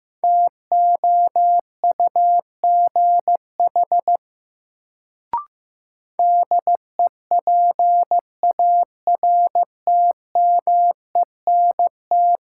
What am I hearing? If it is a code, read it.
Morse code, heard as a keyed tone, about 15 wpm, 700 Hz. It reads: TOUGH DEPARTMENT
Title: Sentences from Top 1200 Words - New Word Drill Alt - 15wpm